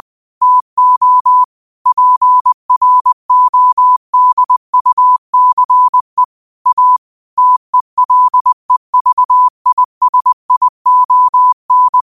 Morse code, heard as a keyed tone, about 20 wpm, 1000 Hz. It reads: TO PRODUCE A TELEVISION
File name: qotdfiles / 20wpm